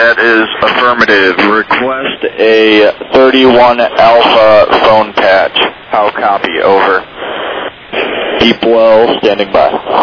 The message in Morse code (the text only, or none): none